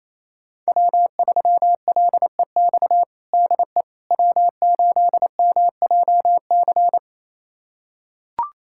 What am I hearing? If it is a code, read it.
Morse code, heard as a keyed tone, about 28 wpm, 700 Hz. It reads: W3LEX DE W8MJC
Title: USA Callsign DE Callsign - 5 Characters - 28wpm